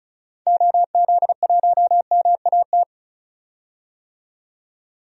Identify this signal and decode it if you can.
Morse code, heard as a keyed tone, about 34 wpm, 700 Hz. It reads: OZ1MAT
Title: DXCC-100 Call Signs - 6 Characters - 35wpm